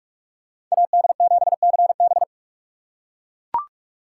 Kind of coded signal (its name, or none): Morse code